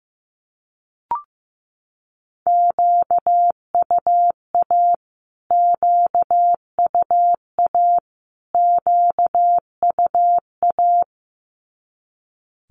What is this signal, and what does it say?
Morse code, heard as a keyed tone, about 15 wpm, 700 Hz. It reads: QUA QUA QUA
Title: Q-Codes Repeated 3x 15wpm